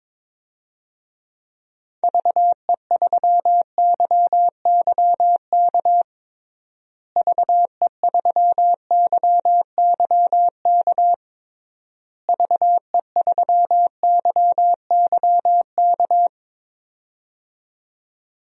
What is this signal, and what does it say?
Morse code, heard as a keyed tone, about 22 wpm, 700 Hz. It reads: VE3YYK VE3YYK VE3YYK
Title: DXCC-100 Call Signs - 6 Characters - Repeated 3x - 22wpm